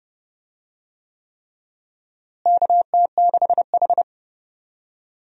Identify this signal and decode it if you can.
Morse code, heard as a keyed tone, about 30 wpm, 700 Hz. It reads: KT6H
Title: USA Call Signs - 4 Characters - Repeated 3x - 30wpm